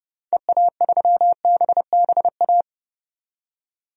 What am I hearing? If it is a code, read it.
Morse code, heard as a keyed tone, about 30 wpm, 700 Hz. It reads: EA3BBA
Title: DXCC-100 Call Signs - 6 Characters - 30wpm